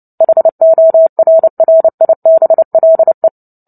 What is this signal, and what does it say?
Morse code, heard as a keyed tone, about 30 wpm, 650 Hz. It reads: HORRIBLE
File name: Book Two / 03-Chapter 3 - 29 WPM